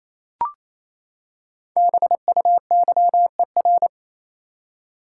Morse code, heard as a keyed tone, about 28 wpm, 700 Hz. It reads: BUYER
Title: Five Letter Words - 28wpm